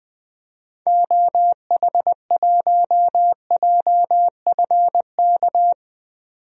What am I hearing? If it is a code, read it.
Morse code, heard as a keyed tone, about 20 wpm, 700 Hz. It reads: OH1JFK
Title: DXCC-100 Call Signs - 6 Characters - 20wpm